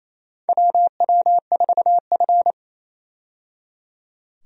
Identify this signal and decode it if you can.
Morse code, heard as a keyed tone, about 28 wpm, 700 Hz. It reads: WW4F
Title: USA Call Signs - 4 Characters - 28wpm